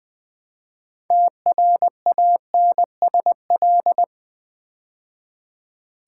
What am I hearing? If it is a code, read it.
Morse code, heard as a keyed tone, about 20 wpm, 700 Hz. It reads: TRANSL
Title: Build A Word - 3 Letters At A Time - 9-Letter Words - 20wpm